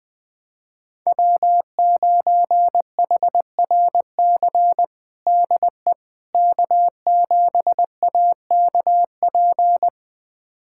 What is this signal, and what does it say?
Morse code, heard as a keyed tone, about 20 wpm, 700 Hz. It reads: W9HRC DE K7AKP
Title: USA Callsign DE Callsign - 5 Characters - 20wpm